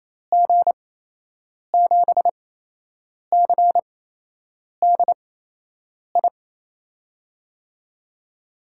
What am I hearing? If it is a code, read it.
Morse code, heard as a keyed tone, about 28 wpm, spaced as for 8 wpm, 700 Hz. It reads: G7CDI
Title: USA Call Signs - 6 Characters - 8x Character Spacing - 28wpm